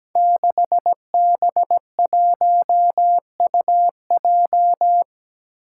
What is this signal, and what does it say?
Morse code, heard as a keyed tone, about 17 wpm, 700 Hz. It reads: 6B1UJ
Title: Five Challenging Letter-Number Combinations 17wpm